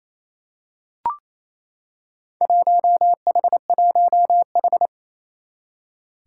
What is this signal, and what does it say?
Morse code, heard as a keyed tone, about 28 wpm, 700 Hz. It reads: 1H1H
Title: Four Challenging Letter-Number Combinations 28wpm